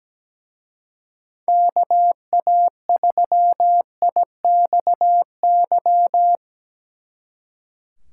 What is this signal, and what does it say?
Morse code, heard as a keyed tone, about 20 wpm, 700 Hz. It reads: KA3IXY
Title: Speed Racing - USA 2x3 Call Signs 17wpm